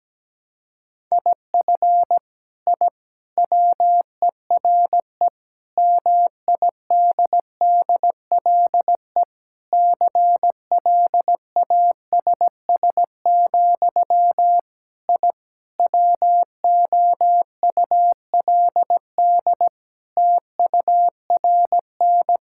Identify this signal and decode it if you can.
Morse code, heard as a keyed tone, about 17 wpm, 700 Hz. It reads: IF I WERE MIDDLE CLASS, I WOULD TURN
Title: Sentences from Top 1000 Words - Review Part 6 - 17wpm